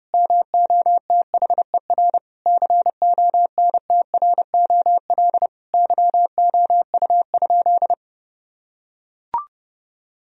Morse code, heard as a keyed tone, about 30 wpm, 700 Hz. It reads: MOTHER CONTROL YOU?
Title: Sentences from Top 1000 Words - Review Part 3 - 30wpm